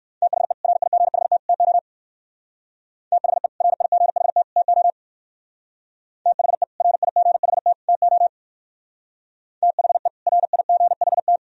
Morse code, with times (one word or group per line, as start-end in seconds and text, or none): 0.2-0.5 THE
0.6-1.4 RIGHT
1.5-1.8 TO
3.1-3.5 THE
3.6-4.4 RIGHT
4.6-4.9 TO
6.3-6.6 THE
6.8-7.7 RIGHT
7.9-8.3 TO
9.6-10.1 THE
10.3-11.4 RIGHT